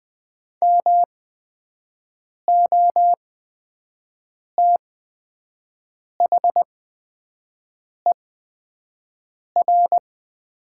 Morse code, heard as a keyed tone, about 20 wpm, spaced as for 5 wpm, 700 Hz. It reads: MOTHER
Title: 8x Character Spacing - Top 200 Words - 20wpm